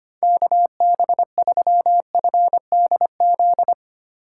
Morse code, heard as a keyed tone, about 25 wpm, 700 Hz. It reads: KB3FDZ
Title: USA Call Signs - 6 Characters - 4x Character Spacing - 25wpm